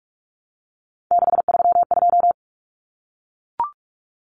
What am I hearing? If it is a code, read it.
Morse code, heard as a keyed tone, about 44 wpm, 700 Hz. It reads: 632
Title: Numbers - 3-Digit 45wpm